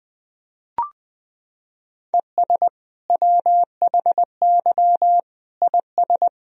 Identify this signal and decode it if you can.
Morse code, heard as a keyed tone, about 20 wpm, 700 Hz. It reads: ES WHY IS
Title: Sentences from Top 200 Words and Abbreviations 20wpm